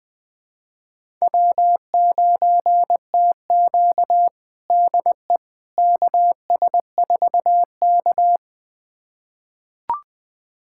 Morse code, heard as a keyed tone, about 20 wpm, 700 Hz. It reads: W9TQ DE KS4K
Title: USA Callsign DE Callsign - 4 Characters - 20wpm